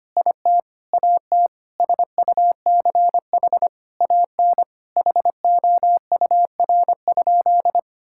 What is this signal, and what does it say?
Morse code, heard as a keyed tone, about 25 wpm, 700 Hz. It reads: IT AT SUCH AN HOUR?
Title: Sentences from Top 1000 Words - Review Part 5 - 25wpm